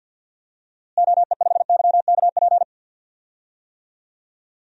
Morse code, heard as a keyed tone, about 52 wpm, 700 Hz. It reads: OE5YKP
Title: DXCC-100 Call Signs - 6 Characters - Repeated 2x - 50wpm